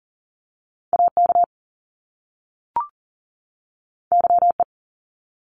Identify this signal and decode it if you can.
Morse code, heard as a keyed tone, about 40 wpm, 700 Hz. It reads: AK YE
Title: Top Two Letter Combinations in Words 40wpm